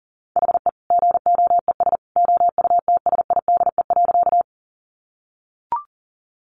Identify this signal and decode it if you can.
Morse code, heard as a keyed tone, about 40 wpm, 700 Hz. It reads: HE GOES OUTSIDE.
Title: Additional Easy Phrases 40wpm